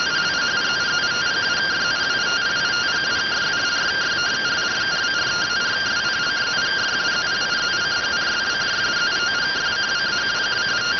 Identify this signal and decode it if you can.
Two-tone teleprinter signal, about 170 Hz shift.